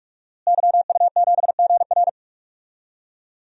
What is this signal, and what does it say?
Morse code, heard as a keyed tone, about 42 wpm, 700 Hz. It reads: YU7GR
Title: DXCC-100 Call Signs - 5 Characters - Repeated 2x - 45wpm